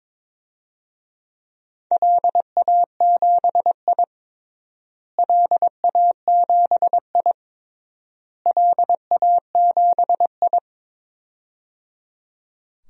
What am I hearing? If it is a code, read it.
Morse code, heard as a keyed tone, about 22 wpm, 700 Hz. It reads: LA7I LA7I LA7I
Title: DXCC-100 Call Signs - 4 Characters - Repeated 3x - 22wpm